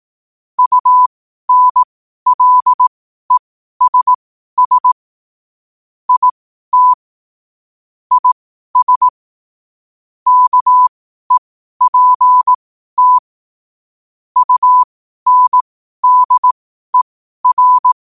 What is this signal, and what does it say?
Morse code, heard as a keyed tone, about 18 wpm, spaced as for 12 wpm, 1000 Hz. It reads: UNLESS IT IS KEPT UNDER
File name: qotdfiles / 12wpm